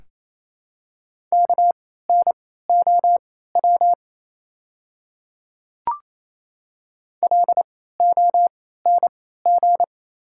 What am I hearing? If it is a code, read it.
Morse code, heard as a keyed tone, about 28 wpm, spaced as for 16 wpm, 700 Hz. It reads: KNOW LONG
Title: 3x Character Spacing - Top 100 Words - 28wpm